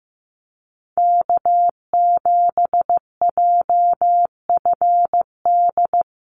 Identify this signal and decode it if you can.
Morse code, heard as a keyed tone, about 15 wpm, 700 Hz. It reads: K7JFD
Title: NA SOTA Chaser Callsigns 15wpm